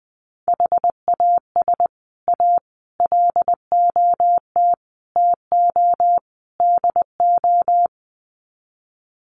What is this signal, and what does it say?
Morse code, heard as a keyed tone, about 20 wpm, 700 Hz. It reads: HAS A LOT TO DO
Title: Sets of 5 Words Encore2 20wpm